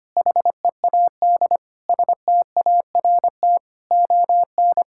Morse code, heard as a keyed tone, about 25 wpm, 700 Hz. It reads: HEAD START ON
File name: Sets of 9 Words - 25wpm